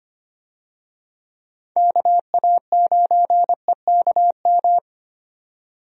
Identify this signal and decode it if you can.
Morse code, heard as a keyed tone, about 25 wpm, 700 Hz. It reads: KA9EKM
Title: USA Call Signs - 6 Characters - 25wpm